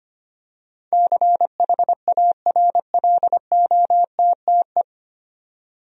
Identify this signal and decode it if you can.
Morse code, heard as a keyed tone, about 25 wpm, 700 Hz. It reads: CHARLOTTE
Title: USA Common Names - 25wpm